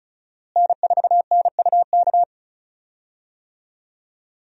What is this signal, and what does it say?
Morse code, heard as a keyed tone, about 36 wpm, 700 Hz. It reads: N4NUK
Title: USA Call Signs - 5 Characters - Repeated 2x - 35wpm